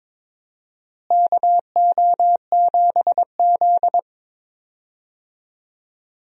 Morse code, heard as a keyed tone, about 22 wpm, 700 Hz. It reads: KO7Z
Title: USA Call Signs - 4 Characters - Repeated 2x - 22wpm